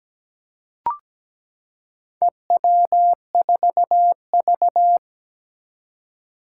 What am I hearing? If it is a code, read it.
Morse code, heard as a keyed tone, about 17 wpm, 700 Hz. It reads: EW4V
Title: Four Challenging Letter-Number Combinations 17wpm